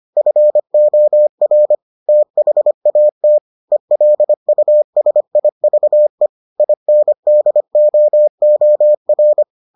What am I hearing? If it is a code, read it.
Morse code, heard as a keyed tone, about 25 wpm, 600 Hz. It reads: FOR THAT ELUSIVE INDOOR